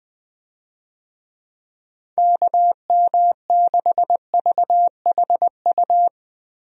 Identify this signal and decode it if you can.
Morse code, heard as a keyed tone, about 21 wpm, 700 Hz. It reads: KM6VHU
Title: USA Call Signs - 6 Characters - Repeated 3x - 20wpm